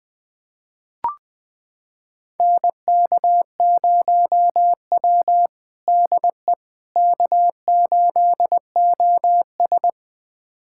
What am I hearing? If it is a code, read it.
Morse code, heard as a keyed tone, about 21 wpm, 700 Hz. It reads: NK0W DE K8OS
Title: USA Callsign DE Callsign - 4 Characters - 20wpm